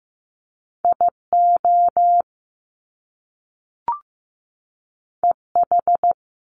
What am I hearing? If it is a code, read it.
Morse code, heard as a keyed tone, about 15 wpm, 700 Hz. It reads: IO EH
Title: Top Two Letter Combinations in Words 15wpm